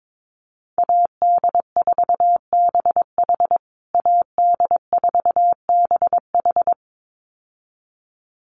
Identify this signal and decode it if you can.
Morse code, heard as a keyed tone, about 22 wpm, 700 Hz. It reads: AD4BH AD4BH
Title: USA Call Signs Repeated 2x 22wpm